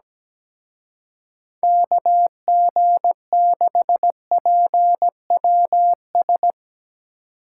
Morse code, heard as a keyed tone, about 18 wpm, 700 Hz. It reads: KG6PWS
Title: Speed Racing - USA 2x3 Call Signs 17wpm